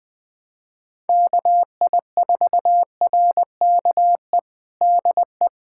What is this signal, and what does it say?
Morse code, heard as a keyed tone, about 20 wpm, 700 Hz. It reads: KI4RKE DE
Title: USA Callsign DE Callsign - 6 Characters - 20wpm